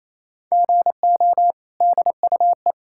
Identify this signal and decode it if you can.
Morse code, heard as a keyed tone, about 28 wpm, 700 Hz. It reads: GO DUE
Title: Sentences from Top 1000 Words - Review Part 6 - 28wpm